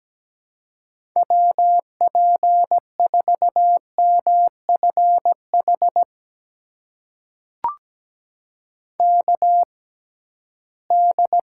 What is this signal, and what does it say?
Morse code, heard as a keyed tone, about 17 wpm, spaced as for 13 wpm, 700 Hz. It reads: WP4MFH KD
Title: USA Call Signs - 6 Characters - 6x Character Spacing - 17wpm